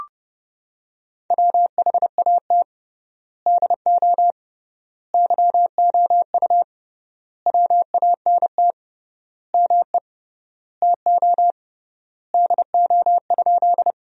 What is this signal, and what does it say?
Morse code, heard as a keyed tone, about 30 wpm, 700 Hz. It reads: WHAT DO YOU WANT ME TO DO?
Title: 3x Word-Spacing - Sentences from Top 200 Words - 30wpm